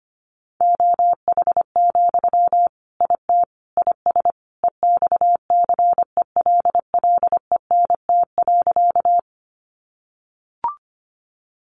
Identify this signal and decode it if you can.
Morse code, heard as a keyed tone, about 25 wpm, 700 Hz. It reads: OH, IT IS EXCELLENT.
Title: Sentences from Top 1000 Words 25wpm